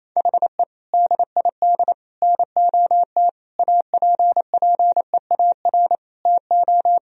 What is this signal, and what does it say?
Morse code, heard as a keyed tone, about 28 wpm, 700 Hz. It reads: HE DID NOT APPEAR TO
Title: Sentences from Top 1000 Words - Review Part 2 - 28wpm